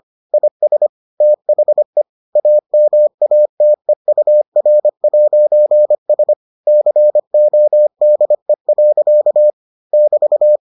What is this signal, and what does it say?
Morse code, heard as a keyed tone, about 25 wpm, 600 Hz. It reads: IS THE AMATEUR'S CODE. <BT>